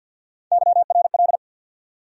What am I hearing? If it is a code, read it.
Morse code, heard as a keyed tone, about 51 wpm, 700 Hz. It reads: YRL
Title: USA Call Sign Postfixes - 3 Letters 50wpm